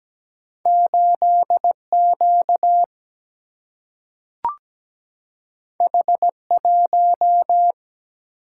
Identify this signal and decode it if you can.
Morse code, heard as a keyed tone, about 17 wpm, 700 Hz. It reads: 8Q H1
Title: Single-Letter Single-Number 17wpm